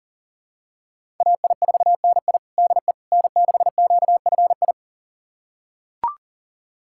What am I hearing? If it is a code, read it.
Morse code, heard as a keyed tone, about 40 wpm, 700 Hz. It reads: AI4NI DE N6QFI
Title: USA Callsign DE Callsign - 5 Characters - 40wpm